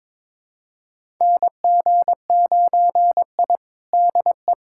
Morse code, heard as a keyed tone, about 23 wpm, 700 Hz. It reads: NG9I DE
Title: USA Callsign DE Callsign - 4 Characters - 22wpm